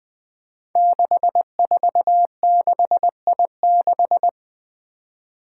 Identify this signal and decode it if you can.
Morse code, heard as a keyed tone, about 20 wpm, 700 Hz. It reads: 646I6
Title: Five Challenging Letter-Number Combinations 20wpm